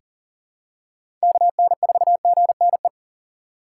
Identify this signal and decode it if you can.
Morse code, heard as a keyed tone, about 40 wpm, 700 Hz. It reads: KN4GNE
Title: USA Call Signs - 6 Characters - 3x Character Spacing - 40wpm